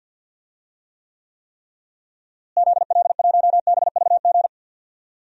Morse code, heard as a keyed tone, about 51 wpm, 700 Hz. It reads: ZL1BVG
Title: DXCC-100 Call Signs - 6 Characters - Repeated 3x - 50wpm